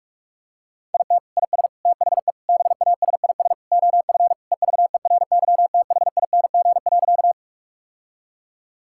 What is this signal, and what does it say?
Morse code, heard as a keyed tone, about 46 wpm, 700 Hz. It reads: IT IS THE BASIS OF EVERYTHING.
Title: Sentences from Top 1000 Words - Review Part 5 - 45wpm